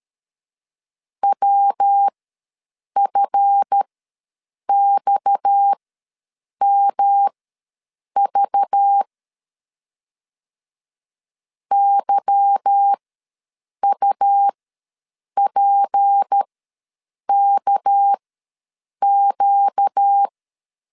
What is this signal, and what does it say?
Morse code, heard as a keyed tone, about 13 wpm, spaced as for 7 wpm, 800 Hz. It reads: WFXMV YUPKQ
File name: LEZIONI / 8 YJ45 / 8 Y ws 12wpm 3409